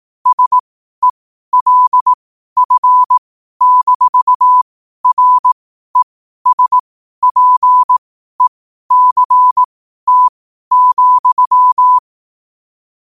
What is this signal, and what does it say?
Morse code, heard as a keyed tone, about 18 wpm, spaced as for 12 wpm, 1000 Hz. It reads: SELF-RESPECT,
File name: qotdfiles / 12wpm